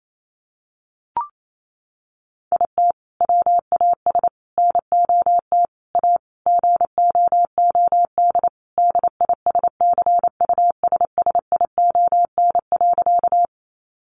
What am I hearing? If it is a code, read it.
Morse code, heard as a keyed tone, about 28 wpm, 700 Hz. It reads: IT WAS NOT A GOOD DISCUSSION.
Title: Sentences from Top 1100 Words 28wpm